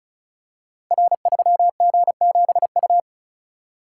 Morse code, heard as a keyed tone, about 36 wpm, 700 Hz. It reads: R3G7U
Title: Code Groups - 5 Letter:Numbers - 35wpm